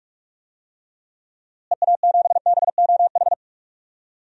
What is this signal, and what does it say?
Morse code, heard as a keyed tone, about 44 wpm, 700 Hz. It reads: EA7BOH
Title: DXCC-100 Call Signs - 6 Characters - Repeated 3x - 45wpm